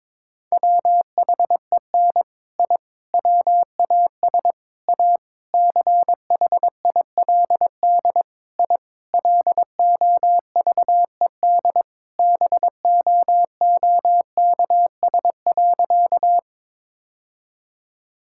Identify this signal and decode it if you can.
Morse code, heard as a keyed tone, about 22 wpm, 700 Hz. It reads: WHEN I WAS A CHILD I LOVED BOOKS.
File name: practice / 22wpm / Sentences with 3-Word Phrases - 22wpm